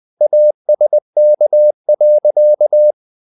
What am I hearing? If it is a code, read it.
Morse code, heard as a keyed tone, about 20 wpm, 600 Hz. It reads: ASK.
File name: the old man and the sea / the old man and the sea part 1 20wpm